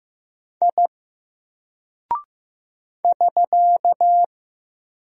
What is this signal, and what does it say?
Morse code, heard as a keyed tone, about 15 wpm, 700 Hz. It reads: I <SK>
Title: Single Character (Letter:Number:Punctuation:Prosign) 15wpm